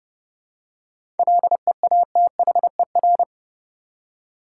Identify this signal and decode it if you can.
Morse code, heard as a keyed tone, about 30 wpm, 700 Hz. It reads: LEATHER
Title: Seven Letter Words - 30wpm